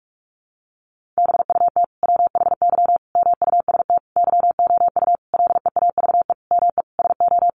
Morse code, heard as a keyed tone, about 44 wpm, 700 Hz. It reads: BUT WHY MUST YOU LEAVE ME SO
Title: Sentences from Top 400 Words 45wpm